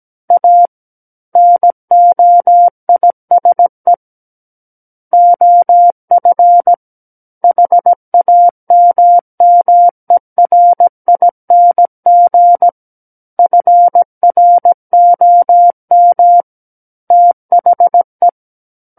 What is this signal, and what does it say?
Morse code, heard as a keyed tone, about 17 wpm, 700 Hz. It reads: A NOISE OF HAMMERING FROM THE
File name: Book One / 08-Chapter 8 - 17 WPM